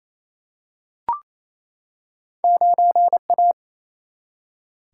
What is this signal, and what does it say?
Morse code, heard as a keyed tone, about 28 wpm, 700 Hz. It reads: 9A
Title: Single-Letter Single-Number 28wpm